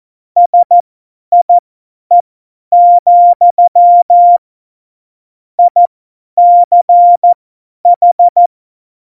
Morse code, heard as a keyed tone, about 14 wpm, spaced as for 10 wpm, 700 Hz. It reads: SIE, ICH
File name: Morsekurs / Witze / Witz 30. Oktober 2018, 50 F70